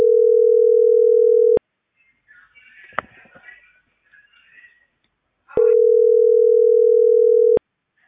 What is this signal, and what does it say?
Ringback tone (North America), 440 + 480 Hz.